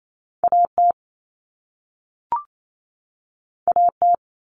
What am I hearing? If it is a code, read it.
Morse code, heard as a keyed tone, about 28 wpm, 700 Hz. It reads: AT AT
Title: Common Cut Numbers - 2-Digit 28wpm